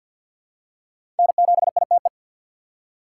Morse code, heard as a keyed tone, about 52 wpm, 700 Hz. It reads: N7ITE
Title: USA Call Signs - 5 Characters - 6x Character Spacing - 50wpm